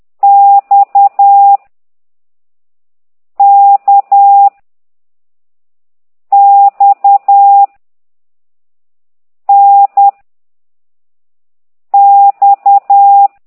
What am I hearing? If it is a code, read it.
Morse code, heard as a keyed tone, about 10 wpm, 800 Hz. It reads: X K X N X